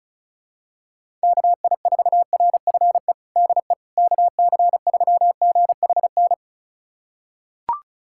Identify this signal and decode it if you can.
Morse code, heard as a keyed tone, about 36 wpm, 700 Hz. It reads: KI4RFE DE KC3GHN
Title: USA Callsign DE Callsign - 6 Characters - 35wpm